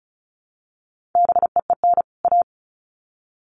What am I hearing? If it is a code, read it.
Morse code, heard as a keyed tone, about 34 wpm, 700 Hz. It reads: BEEN A
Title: Sets of 2 Words 35wpm